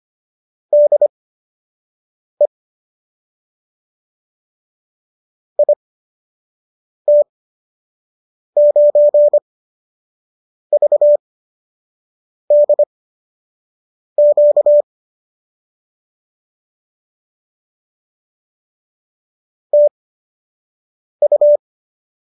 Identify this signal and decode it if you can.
Morse code, heard as a keyed tone, about 25 wpm, spaced as for 6 wpm, 590 Hz. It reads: DE IT9VDQ TU